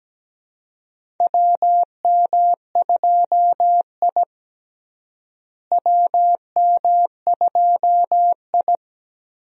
Morse code, heard as a keyed tone, about 17 wpm, 700 Hz. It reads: WM2I WM2I
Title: USA Call Signs - 4 Characters - Repeated 2x - 17wpm